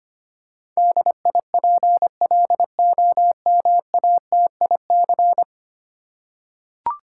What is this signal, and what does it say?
Morse code, heard as a keyed tone, about 25 wpm, 700 Hz. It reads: DIPLOMATIC
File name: 10 Letter Words - First 3 Letters Given - 25wpm